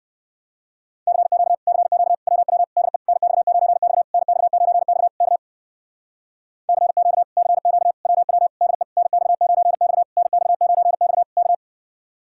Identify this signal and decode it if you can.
Morse code, heard as a keyed tone, about 67 wpm, 700 Hz. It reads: CQ CQ WW DE NX9X NX9X K CQ CQ WW DE NX9X NX9X K